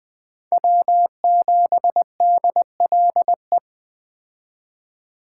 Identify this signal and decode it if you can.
Morse code, heard as a keyed tone, about 21 wpm, 700 Hz. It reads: W7DLE
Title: Speed Racing - USA Call Signs 17wpm